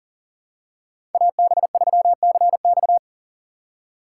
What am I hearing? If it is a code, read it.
Morse code, heard as a keyed tone, about 40 wpm, 700 Hz. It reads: AB3CX
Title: USA Call Signs - 5 Characters - 3x Character Spacing - 40wpm